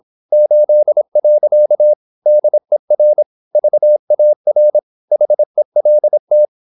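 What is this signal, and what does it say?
Morse code, heard as a keyed tone, about 26 wpm, 600 Hz. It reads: 8. DER VAR HELT
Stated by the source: High-speed (QRQ) Morse at 26 wpm.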